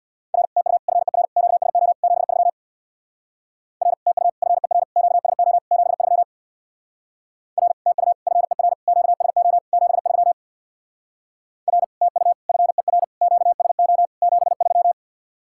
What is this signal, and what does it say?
Morse code, heard as a keyed tone, about 62 wpm, 700 Hz. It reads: R TU FER QSO 73 R TU FER QSO 73 R TU FER QSO 73 R TU FER QSO 73
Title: Speed-Racing Common QSO Phrases - 50wpm